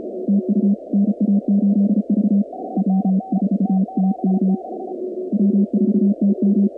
Morse code, heard as a keyed tone, about 34 wpm, 200 Hz. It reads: TU NA8V W4TM W4TM
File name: contests / 2019-cw-wpx-kt5j-k5pi- -2330 0-46